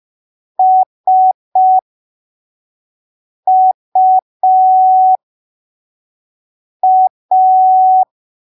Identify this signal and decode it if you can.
Morse code, heard as a keyed tone, about 5 wpm, spaced as for 3 wpm, 750 Hz. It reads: SUA